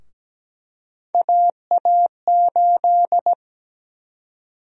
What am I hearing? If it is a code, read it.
Morse code, heard as a keyed tone, about 17 wpm, 700 Hz. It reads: AA8
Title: USA Call Sign Prefixes - Repeated 2x - 17wpm